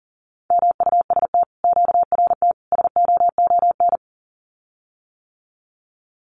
Morse code, heard as a keyed tone, about 39 wpm, 700 Hz. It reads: MUST QRT SOON
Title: Common QSO Phrases Encore 40wpm